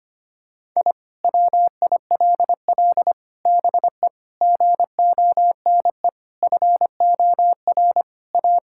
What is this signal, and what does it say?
Morse code, heard as a keyed tone, about 25 wpm, 700 Hz. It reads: I WILL BE GONE FOR A
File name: practice / 25wpm / Sentences with 4-Word Phrases - 25wpm